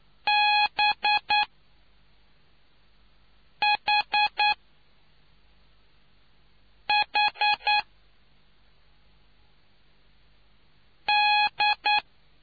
Morse code, heard as a keyed tone, about 9 wpm, 840 Hz. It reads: B H H D